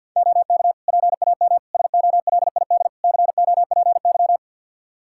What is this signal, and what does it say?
Morse code, heard as a keyed tone, about 51 wpm, 700 Hz. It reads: OK PAM SOLID COPY
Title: Practice QSO - Exchange 3 - More Info - 50wpm